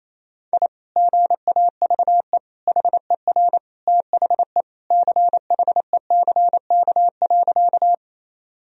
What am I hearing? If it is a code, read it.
Morse code, heard as a keyed tone, about 27 wpm, 700 Hz. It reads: I GAVE HER THE CHECK.
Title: Sentences from Top 1000 Words - Review Part 3 - 28wpm